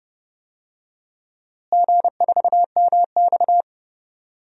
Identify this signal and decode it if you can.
Morse code, heard as a keyed tone, about 30 wpm, 700 Hz. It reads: G4MX
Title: DXCC-100 Call Signs - 4 Characters - Repeated 3x - 30wpm